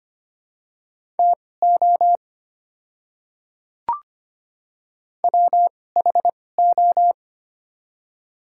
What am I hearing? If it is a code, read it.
Morse code, heard as a keyed tone, about 25 wpm, spaced as for 18 wpm, 700 Hz. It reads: TO WHO
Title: 2x Character Spacing - Top 100 Words - 25wpm